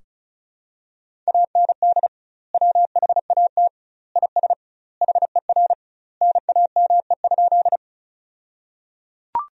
Morse code, heard as a keyed tone, about 36 wpm, 700 Hz. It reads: AND WHAT IS HER NAME?
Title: 2x Word-Spacing - Sentences from Top 200 Words - 35wpm